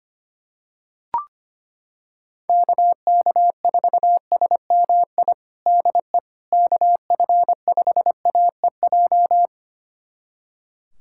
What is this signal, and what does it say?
Morse code, heard as a keyed tone, about 25 wpm, 700 Hz. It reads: KK4SMI DE KF5AEJ
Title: USA Callsign DE Callsign - 6 Characters - 25wpm